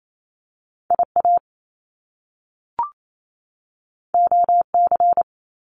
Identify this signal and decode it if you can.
Morse code, heard as a keyed tone, about 28 wpm, 700 Hz. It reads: IA OC
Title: Top Two Letter Combinations in Words 28wpm